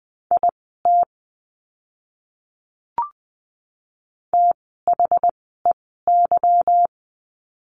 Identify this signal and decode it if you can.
Morse code, heard as a keyed tone, about 20 wpm, spaced as for 14 wpm, 700 Hz. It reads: IT THEY
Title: 185 Lesson 30f - Letter b - Words 20wpm